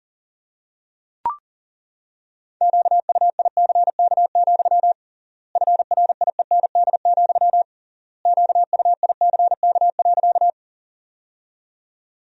Morse code, heard as a keyed tone, about 40 wpm, 700 Hz. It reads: QUICK, FRIEND, QUICK.
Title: 3x Word-Spacing - Sentences from Top 500 Words - 40wpm